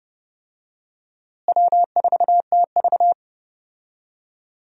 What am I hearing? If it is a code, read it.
Morse code, heard as a keyed tone, about 30 wpm, 700 Hz. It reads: W4TV
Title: USA Call Signs - 4 Characters - Repeated 2x - 30wpm